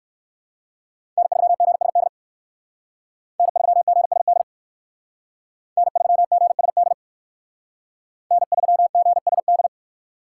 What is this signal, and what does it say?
Morse code, heard as a keyed tone, about 58 wpm, 700 Hz. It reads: N3GSD N3GSD N3GSD N3GSD